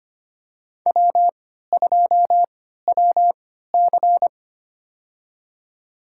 Morse code, heard as a keyed tone, about 25 wpm, spaced as for 14 wpm, 700 Hz. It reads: W2WC
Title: USA Call Signs - 4 Characters - 3x Character Spacing - 25wpm